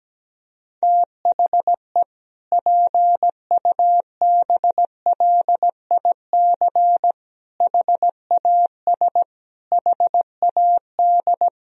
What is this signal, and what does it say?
Morse code, heard as a keyed tone, about 17 wpm, 700 Hz. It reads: THE PUBLIC HAS HAD
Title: Sentences from Top 1000 Words - Review Part 3 - 17wpm